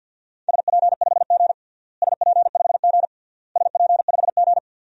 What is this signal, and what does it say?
Morse code, heard as a keyed tone, about 52 wpm, 700 Hz. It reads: SP5G SP5G SP5G
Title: DXCC-100 Call Signs - 4 Characters - Repeated 3x - 50wpm